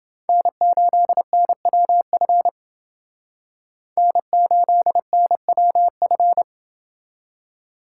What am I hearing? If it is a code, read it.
Morse code, heard as a keyed tone, about 28 wpm, 700 Hz. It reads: N8NWF N8NWF